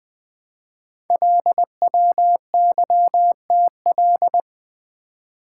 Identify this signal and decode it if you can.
Morse code, heard as a keyed tone, about 20 wpm, 700 Hz. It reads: LWYTL
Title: Code Groups - 5 Letters - 20wpm